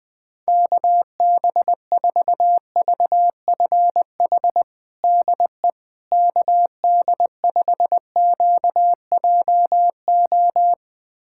Morse code, heard as a keyed tone, about 20 wpm, 700 Hz. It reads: KB4VFH DE KD5QJO
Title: USA Callsign DE Callsign 20wpm